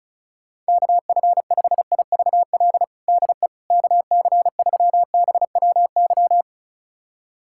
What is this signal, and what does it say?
Morse code, heard as a keyed tone, about 34 wpm, 700 Hz. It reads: KF5IVL DE KC3BWY
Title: USA Callsign DE Callsign - 6 Characters - 35wpm